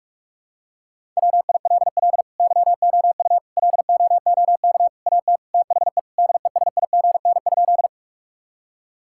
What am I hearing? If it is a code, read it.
Morse code, heard as a keyed tone, about 41 wpm, 700 Hz. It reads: WILL YOU LOOK AT THE DESIGN?
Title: Sentences from Top 1000 Words - Review Part 5 - 45wpm